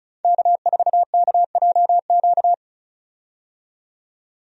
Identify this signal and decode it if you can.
Morse code, heard as a keyed tone, about 36 wpm, 700 Hz. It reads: K4KJQ
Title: USA Call Signs - 5 Characters - 35wpm